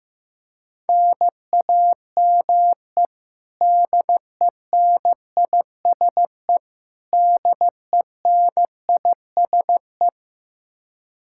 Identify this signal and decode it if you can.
Morse code, heard as a keyed tone, about 16 wpm, 700 Hz. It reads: NAME DENISE DENISE
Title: Practice QSO - Exchange 1 - Beginning - 15wpm